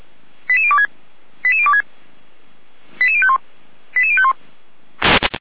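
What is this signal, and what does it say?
Five-tone selective call (ZVEI), heard as quick runs of five tones, tones of about 70 ms. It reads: 88026 88026 88051 88051
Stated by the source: Five-tone ZVEI selective call received by SDR, from the Feuerwehr Orlamünde (fire brigade) server.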